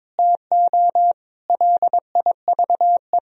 Morse code, heard as a keyed tone, about 22 wpm, 700 Hz. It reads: TO LIVE
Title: Sentences from Top 200 Words and Abbreviations 22wpm